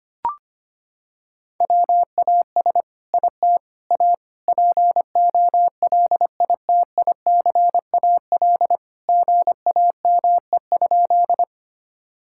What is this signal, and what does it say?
Morse code, heard as a keyed tone, about 25 wpm, 700 Hz. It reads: WAS IT A POLITICAL GAME?
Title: Sentences from Top 1000 Words - Review Part 5 - 25wpm